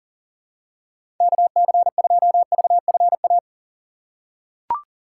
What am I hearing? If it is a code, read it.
Morse code, heard as a keyed tone, about 40 wpm, 700 Hz. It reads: KC2VFA
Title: USA Call Signs - 6 Characters - 2x Character Spacing - 40wpm